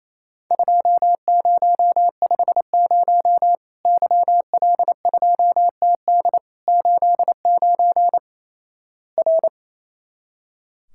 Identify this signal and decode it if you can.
Morse code, heard as a keyed tone, about 28 wpm, 700 Hz. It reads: 2050 YL2TD 89 R
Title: Worked All Europe DX Contest - QTC Exchange - 28wpm